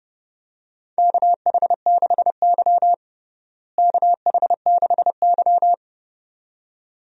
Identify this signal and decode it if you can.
Morse code, heard as a keyed tone, about 32 wpm, 700 Hz. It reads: KH6Y KH6Y
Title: DXCC-100 Call Signs - 4 Characters - Repeated 2x - 30wpm